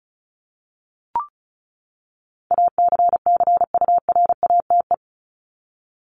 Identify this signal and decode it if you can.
Morse code, heard as a keyed tone, about 34 wpm, 700 Hz. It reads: ACCURATE
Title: Top 1001-1250 Words 35wpm